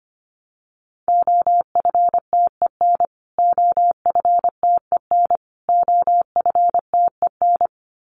Morse code, heard as a keyed tone, about 25 wpm, 700 Hz. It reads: OFTEN OFTEN OFTEN
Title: Top 500 Words Repeated 25wpm